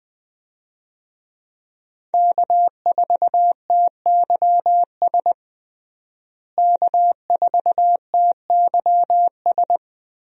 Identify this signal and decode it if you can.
Morse code, heard as a keyed tone, about 20 wpm, 700 Hz. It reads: K4TYS K4TYS
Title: USA Call Signs - 5 Characters - Repeated 2x - 20wpm